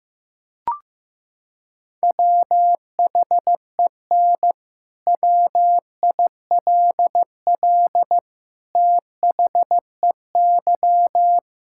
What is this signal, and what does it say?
Morse code, heard as a keyed tone, about 15 wpm, 700 Hz. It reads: WHEN WILL THEY
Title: Sentences from Top 1600 Words 15wpm